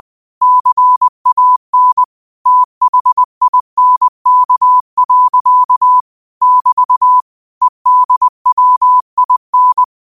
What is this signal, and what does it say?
Morse code, heard as a keyed tone, about 20 wpm, 1000 Hz. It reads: CAN THINK. <BT> EDWIN
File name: qotdfiles / 20wpm